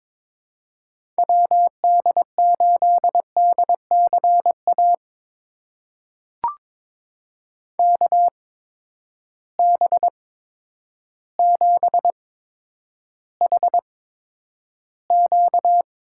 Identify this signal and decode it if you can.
Morse code, heard as a keyed tone, about 22 wpm, spaced as for 10 wpm, 700 Hz. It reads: WD8DCA KB7HQ
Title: USA Call Signs - 6 Characters - 8x Character Spacing - 22wpm